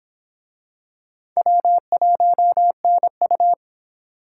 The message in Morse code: W1NU

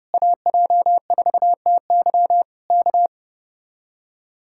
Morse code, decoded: AJ4TY K

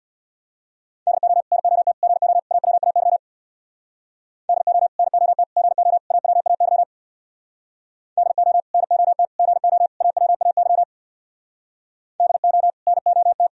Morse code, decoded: DO NOT GO AWAY DO NOT GO AWAY DO NOT GO AWAY DO NOT

